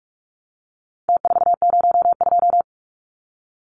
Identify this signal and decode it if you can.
Morse code, heard as a keyed tone, about 44 wpm, 700 Hz. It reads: T402
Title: Numbers - Contest Style - 4-Digit 45wpm